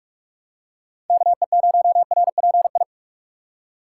Morse code, heard as a keyed tone, about 42 wpm, 700 Hz. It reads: KE0RPI
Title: USA Call Signs - 4x Character Spacing - 45wpm